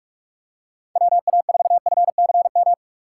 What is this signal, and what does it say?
Morse code, heard as a keyed tone, about 46 wpm, 700 Hz. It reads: WA4FCM
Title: USA Call Signs - 6x Character Spacing - 45wpm